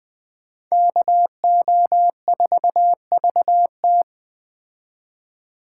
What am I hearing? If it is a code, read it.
Morse code, heard as a keyed tone, about 20 wpm, 700 Hz. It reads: KO4VT
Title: Speed Racing - USA 2x2 Call Signs 17wpm